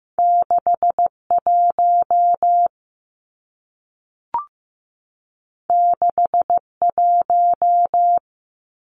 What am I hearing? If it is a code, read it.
Morse code, heard as a keyed tone, about 15 wpm, 700 Hz. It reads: 61 61
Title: Numbers - 2-Digit 15wpm